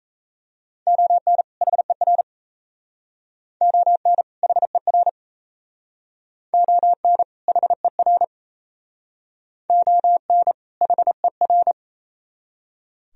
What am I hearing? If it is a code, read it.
Morse code, heard as a keyed tone, about 35 wpm, 700 Hz. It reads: ON HER ON HER ON HER ON HER